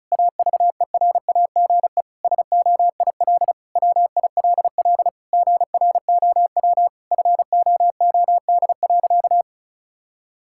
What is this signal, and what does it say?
Morse code, heard as a keyed tone, about 34 wpm, 700 Hz. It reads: AVERAGE SOIL WILL GROW FOOD.
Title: Sentences from Top 1000 Words - Review Part 6 - 35wpm